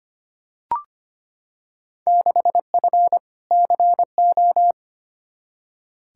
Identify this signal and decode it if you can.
Morse code, heard as a keyed tone, about 25 wpm, 700 Hz. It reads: 6F CO
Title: Field Day Exchanges Encore - 25wpm